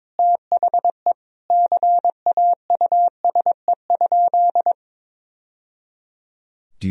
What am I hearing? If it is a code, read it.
Morse code, heard as a keyed tone, about 22 wpm, 700 Hz. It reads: THE CAUSE?
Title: Sentences from Top 1000 Words - Review Part 1 - 22wpm